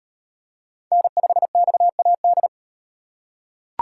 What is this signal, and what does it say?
Morse code, heard as a keyed tone, about 34 wpm, 700 Hz. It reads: N5XAD